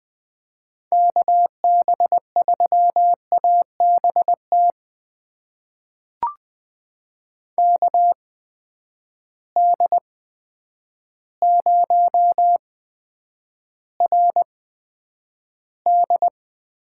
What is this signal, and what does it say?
Morse code, heard as a keyed tone, about 20 wpm, spaced as for 9 wpm, 700 Hz. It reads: KB3ABT KD0RD
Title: USA Call Signs - 6 Characters - 8x Character Spacing - 20wpm